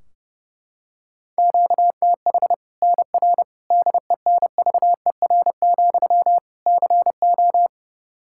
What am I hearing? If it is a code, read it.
Morse code, heard as a keyed tone, about 32 wpm, 700 Hz. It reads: QTH NR DENVER, CO
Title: Practice QSO - Exchange 1 - Beginning - 30wpm